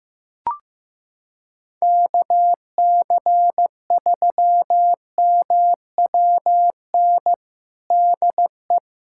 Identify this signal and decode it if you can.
Morse code, heard as a keyed tone, about 15 wpm, 700 Hz. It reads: KC3MWN DE
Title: USA Callsign DE Callsign - 6 Characters - 15wpm